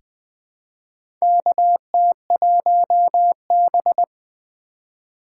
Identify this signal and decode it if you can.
Morse code, heard as a keyed tone, about 20 wpm, 700 Hz. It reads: KT1B
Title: K1USN SST QSO Practice - 20wpm